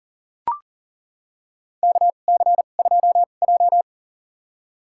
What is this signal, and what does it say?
Morse code, heard as a keyed tone, about 40 wpm, spaced as for 29 wpm, 690 Hz. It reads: KC2J
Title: USA Call Signs - 4 Characters - 2x Character Spacing - 40wpm